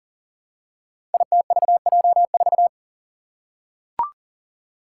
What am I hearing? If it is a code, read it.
Morse code, heard as a keyed tone, about 39 wpm, 700 Hz. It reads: ITVJ4
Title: Five Challenging Letter-Number Combinations 40wpm